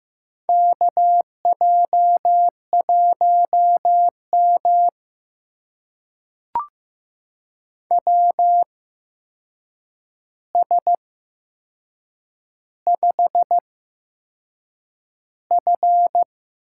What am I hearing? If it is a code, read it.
Morse code, heard as a keyed tone, about 15 wpm, spaced as for 6 wpm, 700 Hz. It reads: KJ1M WS5F
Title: USA Call Signs - 4 Characters - 8x Character Spacing - 15wpm